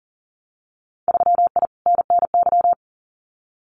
Morse code, heard as a keyed tone, about 40 wpm, 700 Hz. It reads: 3I NNY
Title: Winter Field Day Exchanges 40wpm